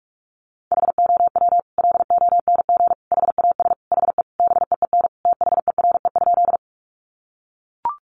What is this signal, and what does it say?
Morse code, heard as a keyed tone, about 44 wpm, 700 Hz. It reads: HOW LONG HAS HE BEEN THERE?
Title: Sentences from Top 100 Words 45wpm